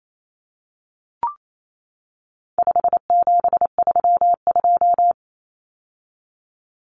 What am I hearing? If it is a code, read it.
Morse code, heard as a keyed tone, about 28 wpm, 700 Hz. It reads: 5732
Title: Numbers - 4-Digit 28wpm